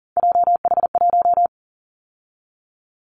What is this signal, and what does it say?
Morse code, heard as a keyed tone, about 40 wpm, 700 Hz. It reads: JH1
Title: Callsign Prefixes for Top 25 DXCC - 40wpm